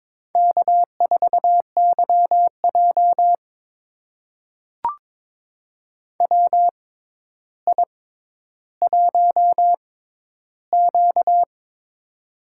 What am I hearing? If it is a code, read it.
Morse code, heard as a keyed tone, about 22 wpm, spaced as for 11 wpm, 700 Hz. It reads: K4YJ WI1Q
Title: USA Call Signs - 4 Characters - 6x Character Spacing - 22wpm